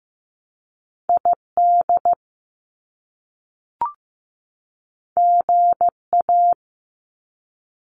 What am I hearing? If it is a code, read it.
Morse code, heard as a keyed tone, about 15 wpm, 700 Hz. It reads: ID GA
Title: US States 15wpm